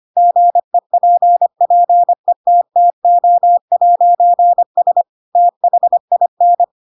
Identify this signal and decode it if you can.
Morse code, heard as a keyed tone, about 25 wpm, 700 Hz. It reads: GEPPETTO'S THIN
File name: Pinocchio / Pinocchio ch2 25 25 ik1hge